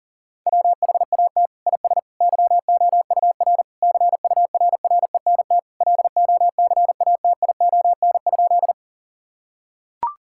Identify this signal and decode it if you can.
Morse code, heard as a keyed tone, about 40 wpm, 700 Hz. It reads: WHAT IS YOUR CURRENT LOCATION?
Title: Sentences from Top 1000 Words - Review Part 6 - 40wpm